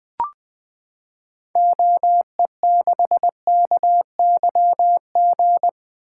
Morse code, heard as a keyed tone, about 20 wpm, 700 Hz. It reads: OE6KYG
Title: 2023 SOTA Activator Call Signs - 20wpm